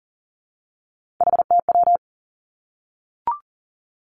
Morse code, heard as a keyed tone, about 40 wpm, 700 Hz. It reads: HTW
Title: Three Challenging Letter-Number Combinations 40wpm